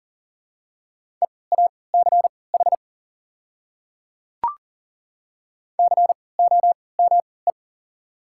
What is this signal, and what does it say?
Morse code, heard as a keyed tone, about 37 wpm, spaced as for 22 wpm, 700 Hz. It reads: EACH COME
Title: 3x Character Spacing - Top 100 Words - 40wpm